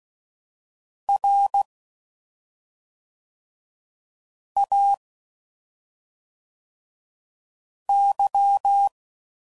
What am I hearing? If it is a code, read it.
Morse code, heard as a keyed tone, about 16 wpm, 800 Hz. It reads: R A Y
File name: Suffixes / Special ING's / SlowToFaster ING's